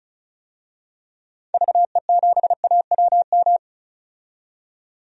Morse code, heard as a keyed tone, about 34 wpm, 700 Hz. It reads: VE7AWM
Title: DXCC-100 Call Signs - 6 Characters - 35wpm